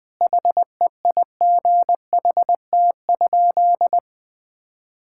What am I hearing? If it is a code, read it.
Morse code, heard as a keyed tone, about 21 wpm, 700 Hz. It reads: HEIGHT?
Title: Sentences from Top 1600 Words 20wpm